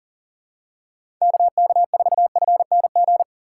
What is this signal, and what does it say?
Morse code, heard as a keyed tone, about 40 wpm, 700 Hz. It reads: KK4FNG
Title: USA Call Signs - 6 Characters - 4x Character Spacing - 40wpm